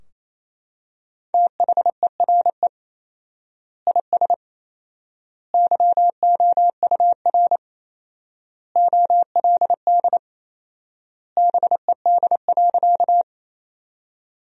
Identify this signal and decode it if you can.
Morse code, heard as a keyed tone, about 28 wpm, 700 Hz. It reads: THERE IS YOUR OLD BED.
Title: 4x Word-Spacing - Sentences from Top 500 Words - 28wpm